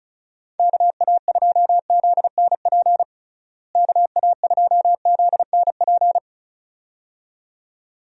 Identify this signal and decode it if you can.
Morse code, heard as a keyed tone, about 34 wpm, 700 Hz. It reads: KA2ZNP KA2ZNP
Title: USA Call Signs - 6 Characters - Repeated 2x - 35wpm